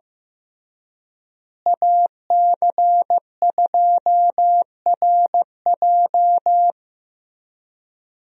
Morse code, heard as a keyed tone, about 16 wpm, 700 Hz. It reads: AC2RJ
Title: USA Call Signs - 5 Characters - Repeated 3x - 15wpm